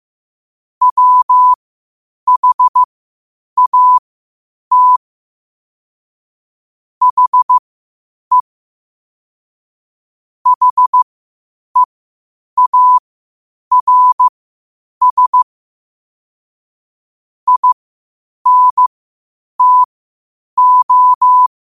Morse code, heard as a keyed tone, about 15 wpm, spaced as for 8 wpm, 1000 Hz. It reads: WHAT HE HEARS INTO
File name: qotdfiles / 8wpm